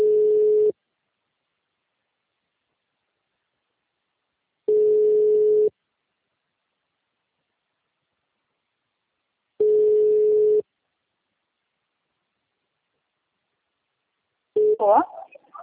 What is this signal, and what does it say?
Ringback tone (Europe, 425 Hz).